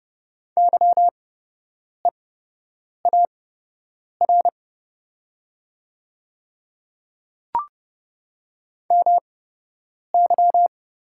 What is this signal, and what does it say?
Morse code, heard as a keyed tone, about 31 wpm, spaced as for 8 wpm, 690 Hz. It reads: YEAR MY
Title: 8x Character Spacing - Top 200 Words - 30wpm